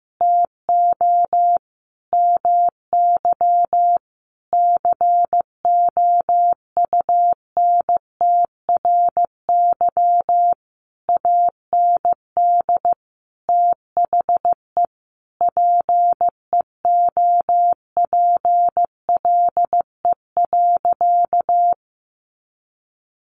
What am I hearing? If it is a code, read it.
Morse code, heard as a keyed tone, about 15 wpm, 700 Hz. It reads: TO MY COUNTRY AND THE PEOPLE.
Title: Sentences from Top 1500 Words 15wpm